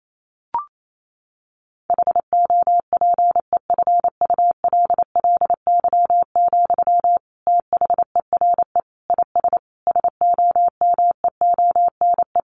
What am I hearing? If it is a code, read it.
Morse code, heard as a keyed tone, about 28 wpm, 700 Hz. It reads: HOPEFULLY, THERE IS SOMEONE
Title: Sentences from Top 1500 Words 28wpm